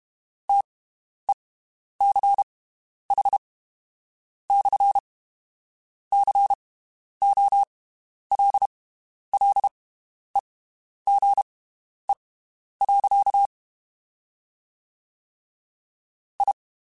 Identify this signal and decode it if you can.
Morse code, heard as a keyed tone, about 32 wpm, spaced as for 10 wpm, 800 Hz. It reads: TECH/COLLEGE. I
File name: (spaced) / QSOs Spaced-mp3s / WD0BGP 1 @30 sp